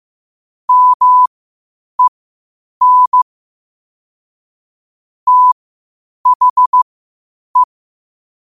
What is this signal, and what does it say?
Morse code, heard as a keyed tone, about 15 wpm, spaced as for 8 wpm, 1000 Hz. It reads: MEN THE
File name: qotdfiles / 8wpm